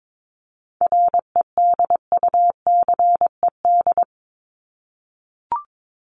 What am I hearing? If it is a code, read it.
Morse code, heard as a keyed tone, about 22 wpm, 700 Hz. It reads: REDUCED
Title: ED Suffix - Top 100 Words with Base Words 22wpm